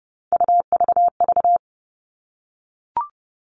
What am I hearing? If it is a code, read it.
Morse code, heard as a keyed tone, about 30 wpm, 700 Hz. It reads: UVV
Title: All Cut Numbers - 3-Digit 30wpm